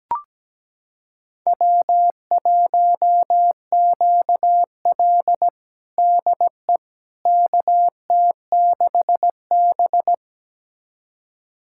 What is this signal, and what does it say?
Morse code, heard as a keyed tone, about 17 wpm, 700 Hz. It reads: W1QL DE KT6B
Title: USA Callsign DE Callsign - 4 Characters - 17wpm